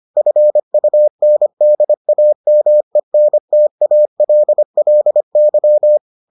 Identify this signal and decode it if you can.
Morse code, heard as a keyed tone, about 25 wpm, 600 Hz. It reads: FUNDAMENTALLY